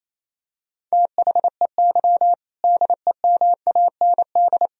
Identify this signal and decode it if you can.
Morse code, heard as a keyed tone, about 27 wpm, 700 Hz. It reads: THEY DEMAND
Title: Sentences from Top 1000 Words - Review Part 3 - 28wpm